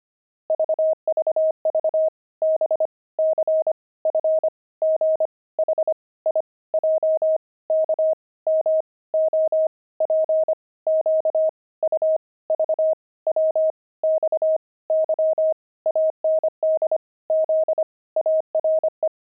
Morse code, heard as a keyed tone, about 25 wpm, 630 Hz. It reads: VVV B C F G H I J K M O P Q U V W X Y AND Z ARE
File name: pangram-fs-25-cs-25